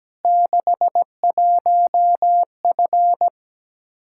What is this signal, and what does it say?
Morse code, heard as a keyed tone, about 17 wpm, 700 Hz. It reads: 61F
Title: Practice QSO - Exchange 2 - Info - 17wpm